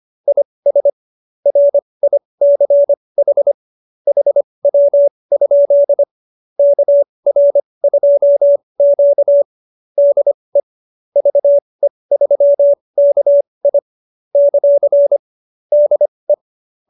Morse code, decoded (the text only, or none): IS RICH HW? KR2Q DE VE3KI ; DE